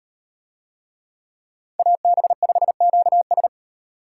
Morse code, heard as a keyed tone, about 38 wpm, 700 Hz. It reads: AB5QS